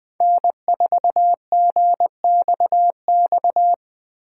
Morse code, heard as a keyed tone, about 20 wpm, 700 Hz. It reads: N4GXX
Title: Speed Racing - USA 1x3 Call Signs 17wpm